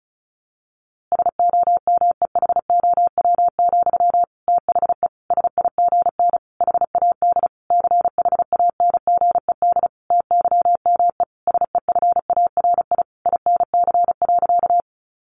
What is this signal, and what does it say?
Morse code, heard as a keyed tone, about 34 wpm, 700 Hz. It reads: SOMEHOW, THE SIGN HAD CHANGED TYME SEFARI INC.
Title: A Sound Of Thunder 35wpm